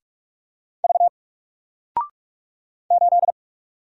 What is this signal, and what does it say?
Morse code, heard as a keyed tone, about 46 wpm, 700 Hz. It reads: V 8
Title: Single Letter-Number 45wpm